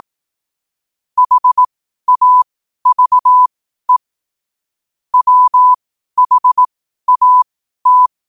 Morse code, heard as a keyed tone, about 18 wpm, spaced as for 12 wpm, 1000 Hz. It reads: HAVE WHAT
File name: qotdfiles / 12wpm